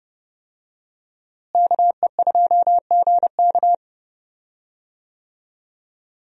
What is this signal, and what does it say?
Morse code, heard as a keyed tone, about 32 wpm, 700 Hz. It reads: KE2GK
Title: USA Call Signs - 5 Characters - Repeated 3x - 30wpm